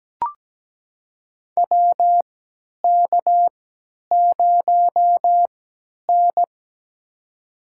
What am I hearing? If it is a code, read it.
Morse code, heard as a keyed tone, about 17 wpm, spaced as for 10 wpm, 700 Hz. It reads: WK0N
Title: USA Call Signs - 4 Characters - 3x Character Spacing - 17wpm